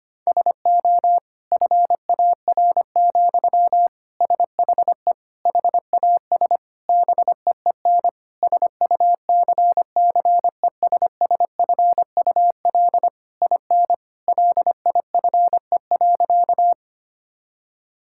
Morse code, heard as a keyed tone, about 25 wpm, 700 Hz. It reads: SO FAR, SHE HAS BEEN SUCCESSFUL IN LIFE.
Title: Sentences from Top 1000 Words - Review Part 6 - 25wpm